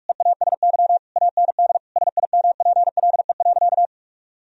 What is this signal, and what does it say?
Morse code, heard as a keyed tone, about 46 wpm, 700 Hz. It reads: EASY AND SIMPLE.
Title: Sentences from Top 1000 Words - Review Part 3 - 45wpm